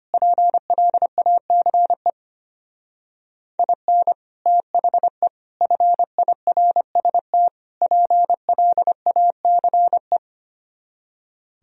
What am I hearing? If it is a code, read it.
Morse code, heard as a keyed tone, about 31 wpm, 700 Hz. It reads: PLACE IN THE FIRST PLACE
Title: Speed-Racing - Sets of 4 Words 25wpm